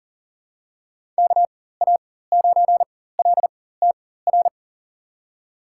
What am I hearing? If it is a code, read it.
Morse code, heard as a keyed tone, about 40 wpm, spaced as for 19 wpm, 700 Hz. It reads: KA9LTR
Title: USA Call Signs - 6 Characters - 4x Character Spacing - 40wpm